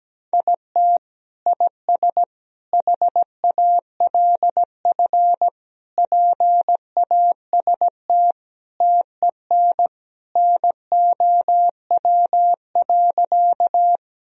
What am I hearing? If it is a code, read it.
Morse code, heard as a keyed tone, about 17 wpm, 700 Hz. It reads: IT IS HALF PAST TEN NOW.
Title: Sentences from Top 1000 Words - Review Part 3 - 17wpm